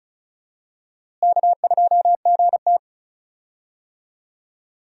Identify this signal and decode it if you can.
Morse code, heard as a keyed tone, about 34 wpm, 700 Hz. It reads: K2GT
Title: USA Call Signs - 4 Characters - Repeated 3x - 35wpm